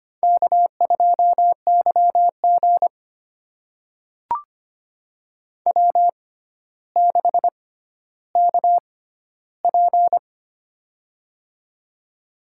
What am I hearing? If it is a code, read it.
Morse code, heard as a keyed tone, about 25 wpm, spaced as for 13 wpm, 700 Hz. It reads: K2YG W6KP
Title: USA Call Signs - 4 Characters - 6x Character Spacing - 25wpm